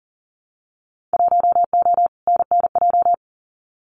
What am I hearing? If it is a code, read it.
Morse code, heard as a keyed tone, about 38 wpm, 700 Hz. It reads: 1O NNJ
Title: Winter Field Day Exchanges 40wpm